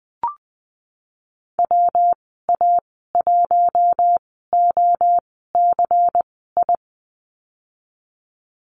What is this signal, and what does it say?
Morse code, heard as a keyed tone, about 20 wpm, spaced as for 15 wpm, 700 Hz. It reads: WA1OCI
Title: 122 Lesson 20g - Number 3 - Call Signs 20wpm